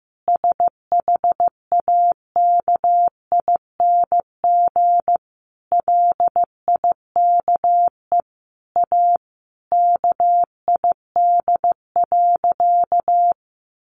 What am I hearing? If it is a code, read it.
Morse code, heard as a keyed tone, about 15 wpm, 700 Hz. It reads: SHAKING LIKE A KID.
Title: A Sound Of Thunder 15wpm